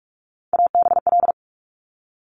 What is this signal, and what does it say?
Morse code, heard as a keyed tone, about 46 wpm, 700 Hz. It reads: ABL
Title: Top Three Letter Combinations in Words 45wpm